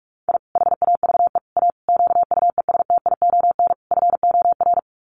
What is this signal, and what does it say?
Morse code, heard as a keyed tone, about 44 wpm, 700 Hz. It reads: I HAVE A QUESTION FOR
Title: Sets of 5 Words Encore2 45wpm